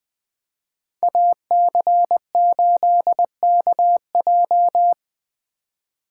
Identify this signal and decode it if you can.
Morse code, heard as a keyed tone, about 20 wpm, 700 Hz. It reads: AC8KJ
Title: USA Call Signs - 2x Character Spacing - 20wpm